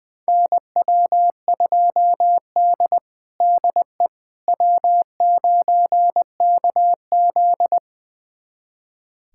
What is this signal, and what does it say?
Morse code, heard as a keyed tone, about 21 wpm, 700 Hz. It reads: NW2D DE W9KZ
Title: USA Callsign DE Callsign - 4 Characters - 20wpm